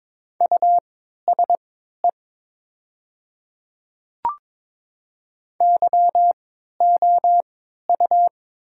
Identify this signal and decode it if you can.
Morse code, heard as a keyed tone, about 22 wpm, spaced as for 13 wpm, 700 Hz. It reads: USE YOU
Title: 3x Character Spacing - Top 100 Words - 22wpm